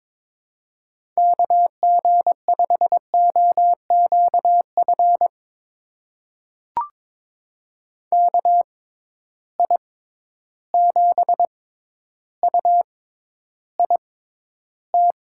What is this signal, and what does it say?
Morse code, heard as a keyed tone, about 22 wpm, spaced as for 11 wpm, 700 Hz. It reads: KG5OQF KI7UIT
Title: USA Call Signs - 6 Characters - 6x Character Spacing - 22wpm